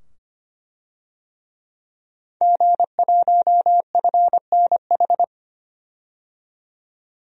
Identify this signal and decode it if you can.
Morse code, heard as a keyed tone, about 25 wpm, 700 Hz. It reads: G1FNH
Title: DXCC-100 Call Signs - 5 Characters - 25wpm